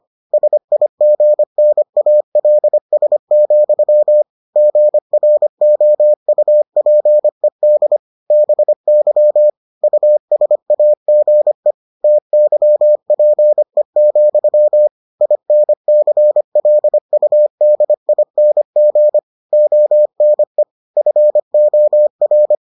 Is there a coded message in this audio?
Yes: Morse code from 0.3 to 22.6 s, heard as a keyed tone, about 25 wpm, 600 Hz. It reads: SIGNALS, GROUPED BY USAGE TYPE, INCLUDING ONE FOR